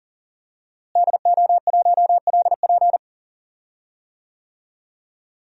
Morse code, heard as a keyed tone, about 38 wpm, 700 Hz. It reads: DO1LP
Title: DXCC-100 Call Signs - 5 Characters - Repeated 2x - 40wpm